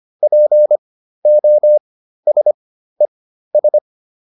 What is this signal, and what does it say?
Morse code, heard as a keyed tone, about 25 wpm, spaced as for 13 wpm, 600 Hz. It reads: POSES